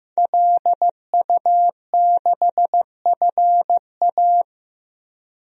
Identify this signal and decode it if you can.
Morse code, heard as a keyed tone, about 15 wpm, 700 Hz. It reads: LU6FA
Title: DXCC-100 Call Signs - 5 Characters - Repeated 2x - 15wpm